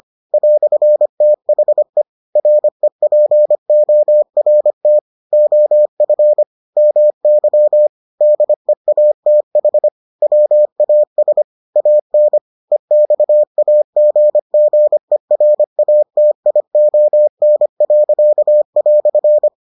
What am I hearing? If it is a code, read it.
Morse code, heard as a keyed tone, about 25 wpm, 600 Hz. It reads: "THE REPORT OF MY DEATH WAS AN EXAGGERATION."